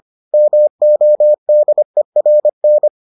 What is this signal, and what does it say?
Morse code, heard as a keyed tone, about 25 wpm, 600 Hz. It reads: MODERN